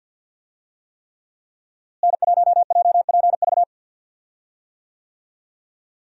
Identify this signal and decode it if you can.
Morse code, heard as a keyed tone, about 53 wpm, 700 Hz. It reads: N1JPV
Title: USA Call Signs - 5 Characters - Repeated 3x - 50wpm